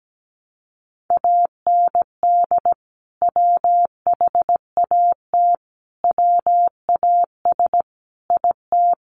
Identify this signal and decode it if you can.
Morse code, heard as a keyed tone, about 17 wpm, 700 Hz. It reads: AND WHAT WAS IT
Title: Sentences from Top 100 Words 17wpm